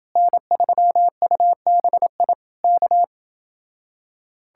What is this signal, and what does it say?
Morse code, heard as a keyed tone, about 29 wpm, 700 Hz. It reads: N3UBI K